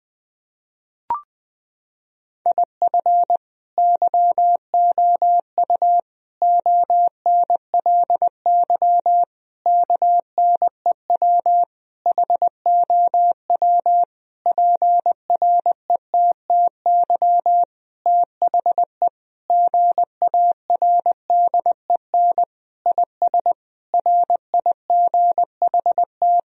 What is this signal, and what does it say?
Morse code, heard as a keyed tone, about 20 wpm, 700 Hz. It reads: IF YOU ONLY KNEW HOW PRETTY THE GARDEN IS RIGHT
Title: Sentences from Top 1000 Words - Review Part 4 - 20wpm